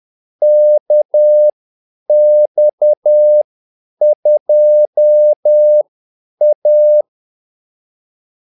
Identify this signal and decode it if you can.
Morse code, heard as a keyed tone, about 10 wpm, spaced as for 8 wpm, 600 Hz. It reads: KX2A
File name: Callsigns-10-8wpm